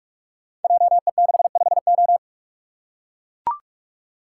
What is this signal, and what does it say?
Morse code, heard as a keyed tone, about 44 wpm, 700 Hz. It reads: JE65O
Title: Code Groups - 5 Letter:Numbers - 45wpm